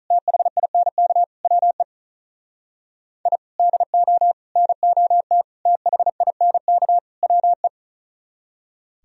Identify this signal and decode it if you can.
Morse code, heard as a keyed tone, about 46 wpm, 700 Hz. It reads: THINK WE I DO NOT THINK WE